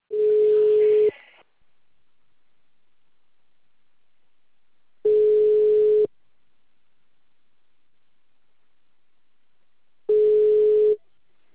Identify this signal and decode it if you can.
Ringback tone (Europe, 425 Hz).
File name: out- -2044- - - .6445